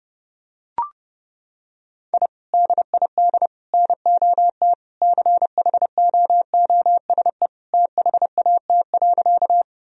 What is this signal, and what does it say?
Morse code, heard as a keyed tone, about 30 wpm, 700 Hz. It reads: I DID NOT CHOOSE THAT.
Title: Sentences from Top 1000 Words - Review Part 4 - 30wpm